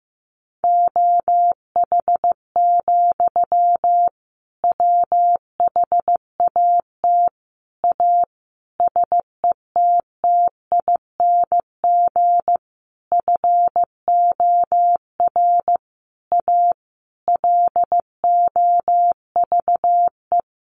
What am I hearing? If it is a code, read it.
Morse code, heard as a keyed tone, about 15 wpm, 700 Hz. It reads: OH, WHAT A SETTING FOR A LOVE
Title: Sentences from Top 1300 Words 15wpm